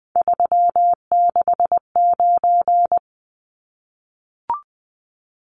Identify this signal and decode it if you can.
Morse code, heard as a keyed tone, about 20 wpm, 700 Hz. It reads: 369
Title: Numbers - 3-Digit 20wpm